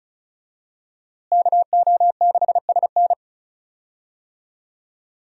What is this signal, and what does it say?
Morse code, heard as a keyed tone, about 36 wpm, 700 Hz. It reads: KO6SN
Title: USA Call Signs - 5 Characters - Repeated 3x - 35wpm